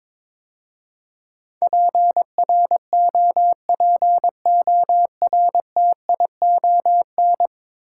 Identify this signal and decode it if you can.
Morse code, heard as a keyed tone, about 22 wpm, 700 Hz. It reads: PROPORTION
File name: practice / 22wpm / 10 Letter Words - First 3 Letters Given - 22wpm